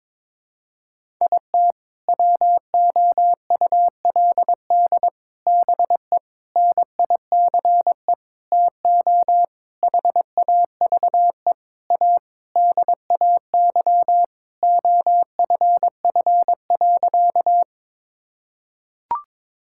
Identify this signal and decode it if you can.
Morse code, heard as a keyed tone, about 22 wpm, 700 Hz. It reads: IT WOULD BE NICE TO HAVE A DAY OFF.
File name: Sentences with 7-Word Phrases - 22wpm